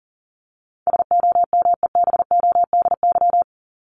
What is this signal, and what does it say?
Morse code, heard as a keyed tone, about 40 wpm, 700 Hz. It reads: SOMEBODY
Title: Top 1751-2000 Words 40wpm